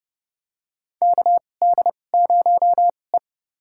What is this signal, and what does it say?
Morse code, heard as a keyed tone, about 30 wpm, spaced as for 22 wpm, 700 Hz. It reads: KD0E
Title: USA Call Signs - 4 Characters - 2x Character Spacing - 30wpm